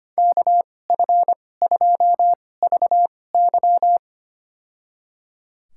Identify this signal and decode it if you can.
Morse code, heard as a keyed tone, about 25 wpm, spaced as for 18 wpm, 700 Hz. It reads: KF2VY
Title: USA Call Signs - 5 Characters - 2x Character Spacing - 25wpm